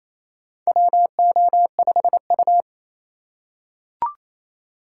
Morse code, heard as a keyed tone, about 28 wpm, 700 Hz. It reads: WO5U
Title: USA Call Signs - 4 Characters - 3x Character Spacing - 28wpm